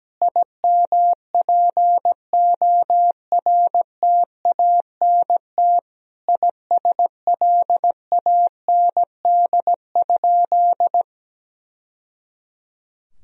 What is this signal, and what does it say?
Morse code, heard as a keyed tone, about 17 wpm, 700 Hz. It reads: IMPORTANT ISLAND?
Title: Sentences from Top 1000 Words - Review Part 3 - 17wpm